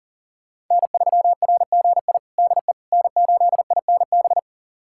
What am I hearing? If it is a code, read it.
Morse code, heard as a keyed tone, about 40 wpm, 700 Hz. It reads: N3RGI DE N8INB
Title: USA Callsign DE Callsign - 5 Characters - 40wpm